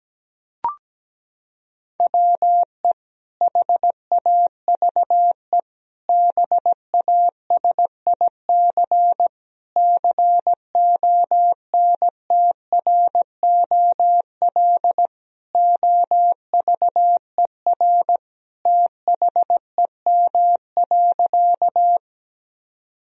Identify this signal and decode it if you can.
Morse code, heard as a keyed tone, about 17 wpm, 700 Hz. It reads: WE HAVE BASIC CONTROL OVER THEM.
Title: Sentences from Top 1000 Words - Review Part 3 - 17wpm